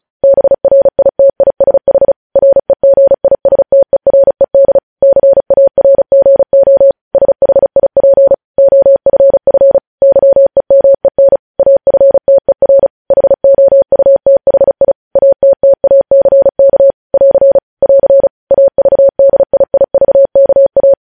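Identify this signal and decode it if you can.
Morse code, heard as a keyed tone, about 36 wpm, 550 Hz. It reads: BRITISH REGISTERED CARGO SHIP OFF YEMEN AFTER HOUTHI ATTACK <AR> <AR> AVDIIVKA
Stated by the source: Morse code news headlines sent at 35 wpm.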